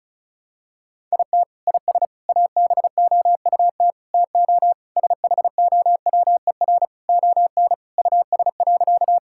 Morse code, heard as a keyed tone, about 34 wpm, 700 Hz. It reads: IT IS ABOUT TO SHOWER ON US.
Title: Sentences from Top 1600 Words 35wpm